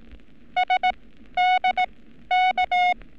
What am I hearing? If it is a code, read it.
Morse code, heard as a keyed tone, about 18 wpm, spaced as for 12 wpm, 730 Hz. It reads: SDK
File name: training / NAVPERS 11224-11235 / NAVPERS 11227 RA RB